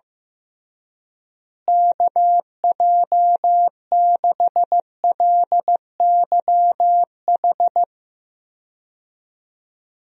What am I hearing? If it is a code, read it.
Morse code, heard as a keyed tone, about 15 wpm, 700 Hz. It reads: KJ6LYH
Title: USA Call Signs - 6 Characters - Repeated 2x - 15wpm